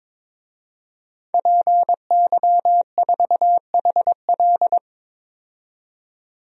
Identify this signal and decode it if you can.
Morse code, heard as a keyed tone, about 21 wpm, 700 Hz. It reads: PY4HL
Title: DXCC-100 Call Signs - 5 Characters - 22wpm